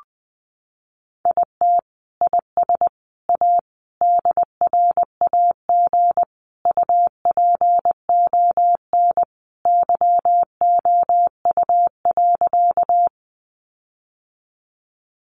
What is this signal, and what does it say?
Morse code, heard as a keyed tone, about 20 wpm, 700 Hz. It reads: IT IS A DRAG UPON YOU.
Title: Sentences from Top 1500 Words 20wpm